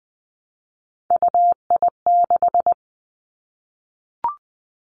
Morse code, heard as a keyed tone, about 20 wpm, 700 Hz. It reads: UI6
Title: Three Challenging Letter-Number Combinations 20wpm